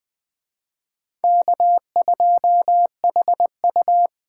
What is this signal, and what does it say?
Morse code, heard as a keyed tone, about 24 wpm, 700 Hz. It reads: K2HU
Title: Speed Racing - USA 1x2 Call Signs 20wpm